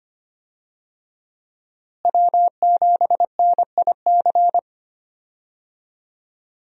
Morse code, heard as a keyed tone, about 25 wpm, 700 Hz. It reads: W7NIC
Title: USA Call Signs - 5 Characters - Repeated 3x - 25wpm